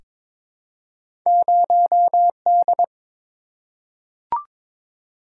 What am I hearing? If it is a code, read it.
Morse code, heard as a keyed tone, about 22 wpm, 700 Hz. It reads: 0D
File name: practice / 22wpm / Two Letter-Number Combinations - 22wpm